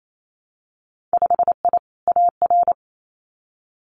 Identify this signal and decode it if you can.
Morse code, heard as a keyed tone, about 28 wpm, 700 Hz. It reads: 5I AR
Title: Winter Field Day Exchanges Encore 28wpm